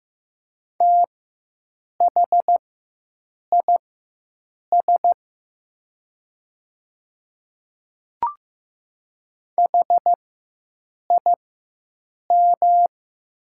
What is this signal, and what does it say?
Morse code, heard as a keyed tone, about 15 wpm, spaced as for 7 wpm, 700 Hz. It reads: THIS HIM
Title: 4x Character Spacing - Top 100 Words - 15wpm